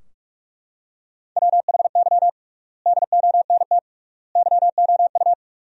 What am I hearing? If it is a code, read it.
Morse code, heard as a keyed tone, about 43 wpm, 700 Hz. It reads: WHY DONT YOU
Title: 3x Word-Spacing - Sentences from Top 300 Words - 45wpm